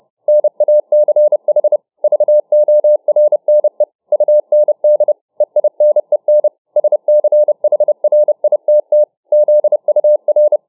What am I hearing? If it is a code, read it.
Morse code, heard as a keyed tone, about 30 wpm, 600 Hz. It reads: NACH VORNE UND EINEN SCHRITT ZUR